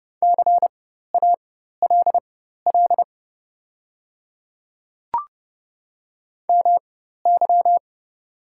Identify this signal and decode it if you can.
Morse code, heard as a keyed tone, about 30 wpm, spaced as for 14 wpm, 700 Hz. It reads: CALL MY
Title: 4x Character Spacing - Top 100 Words - 30wpm